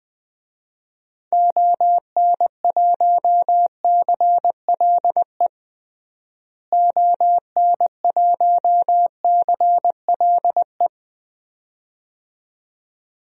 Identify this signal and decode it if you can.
Morse code, heard as a keyed tone, about 20 wpm, 700 Hz. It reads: ON1CLE ON1CLE
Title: DXCC-100 Call Signs - 6 Characters - Repeated 2x - 20wpm